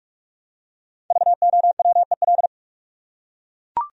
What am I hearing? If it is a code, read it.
Morse code, heard as a keyed tone, about 40 wpm, 700 Hz. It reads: VOWEL